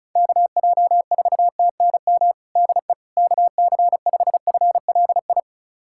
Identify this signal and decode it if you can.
Morse code, heard as a keyed tone, about 36 wpm, 700 Hz. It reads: KJ4TNM DE KC5FLI
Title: USA Callsign DE Callsign - 6 Characters - 35wpm